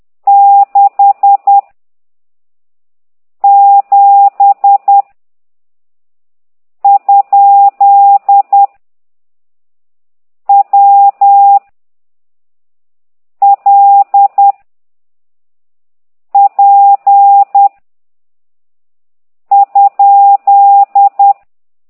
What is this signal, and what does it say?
Morse code, heard as a keyed tone, about 10 wpm, 800 Hz. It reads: 6 7 ? W L P ?
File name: ct2idw / exerc15